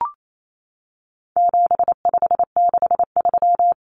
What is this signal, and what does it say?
Morse code, heard as a keyed tone, about 28 wpm, 700 Hz. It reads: 7563
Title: Numbers - 4-Digit 28wpm